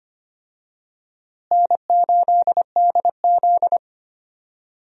Morse code, heard as a keyed tone, about 25 wpm, 700 Hz. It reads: N8DZ
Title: USA Call Signs - 4 Characters - Repeated 3x - 25wpm